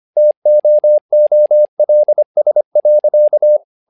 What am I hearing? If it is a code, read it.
Morse code, heard as a keyed tone, about 25 wpm, 600 Hz. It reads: TOOLS.
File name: learning center / tower trouble / tower trouble 25wpm